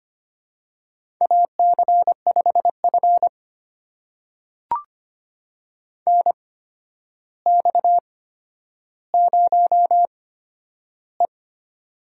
Morse code, heard as a keyed tone, about 25 wpm, spaced as for 11 wpm, 700 Hz. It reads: AC5F NX0E
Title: USA Call Signs - 4 Characters - 8x Character Spacing - 25wpm